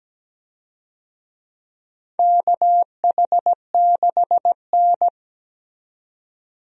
Morse code, heard as a keyed tone, about 17 wpm, 700 Hz. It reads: KH6N
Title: DXCC-100 Call Signs - 4 Characters - 17wpm